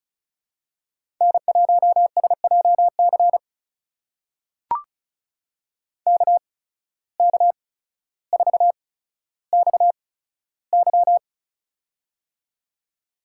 Morse code, heard as a keyed tone, about 34 wpm, spaced as for 15 wpm, 700 Hz. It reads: N1SJC KK4XY
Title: USA Call Signs - 5 Characters - 8x Character Spacing - 35wpm